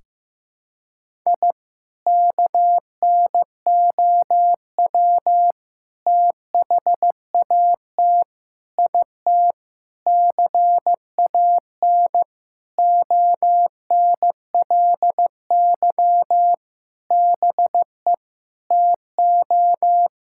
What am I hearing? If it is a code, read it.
Morse code, heard as a keyed tone, about 15 wpm, 700 Hz. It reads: I KNOW THAT IT CAN ONLY BE TO
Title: Sentences from Top 1000 Words - Review Part 6 - 15wpm